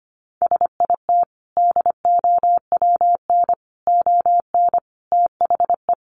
Morse code, heard as a keyed tone, about 25 wpm, 700 Hz. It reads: SIT DOWN ON THE
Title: Sentences from Top 1400 Words 25wpm